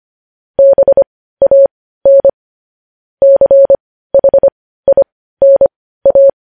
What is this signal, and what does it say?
Morse code, heard as a keyed tone, about 25 wpm, spaced as for 15 wpm, 550 Hz. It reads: BAN CHINA